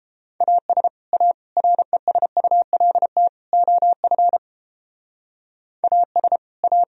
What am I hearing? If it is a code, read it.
Morse code, heard as a keyed tone, about 29 wpm, 700 Hz. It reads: AS A RESULT OF AS A